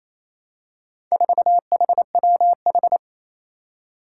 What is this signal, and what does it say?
Morse code, heard as a keyed tone, about 27 wpm, 700 Hz. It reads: 4HWH
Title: Four Challenging Letter-Number Combinations 28wpm